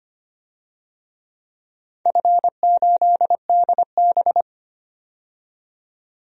Morse code, heard as a keyed tone, about 25 wpm, 700 Hz. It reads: F8DB
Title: DXCC-100 Call Signs - 4 Characters - 25wpm